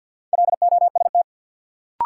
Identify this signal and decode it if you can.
Morse code, heard as a keyed tone, about 52 wpm, 700 Hz. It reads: LOST
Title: Sentences from Top 1400 Words - New Word Drill Alt - 50wpm